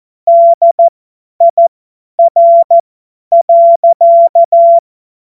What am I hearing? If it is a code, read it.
Morse code, heard as a keyed tone, about 14 wpm, spaced as for 10 wpm, 680 Hz. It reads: DIR.
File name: Morsekurs / Witze / Witz 29. Juni 2021, 50 F70